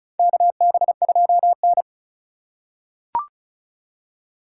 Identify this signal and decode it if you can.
Morse code, heard as a keyed tone, about 36 wpm, 700 Hz. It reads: KB2N
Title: USA Call Signs - 3x Character Spacing - 35wpm